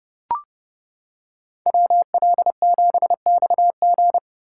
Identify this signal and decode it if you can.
Morse code, heard as a keyed tone, about 29 wpm, 700 Hz. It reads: WL7XG